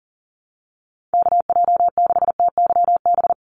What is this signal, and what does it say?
Morse code, heard as a keyed tone, about 40 wpm, 700 Hz. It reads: KJ6TYB
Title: USA Call Signs 40wpm